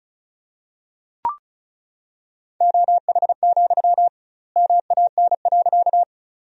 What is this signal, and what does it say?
Morse code, heard as a keyed tone, about 34 wpm, 700 Hz. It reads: OH, MAN.
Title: 2x Word-Spacing - Sentences from Top 500 Words - 35wpm